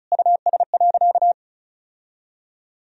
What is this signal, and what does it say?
Morse code, heard as a keyed tone, about 36 wpm, 700 Hz. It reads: US.
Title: Sentences from Top 1000 Words - Review Part 1 - 35wpm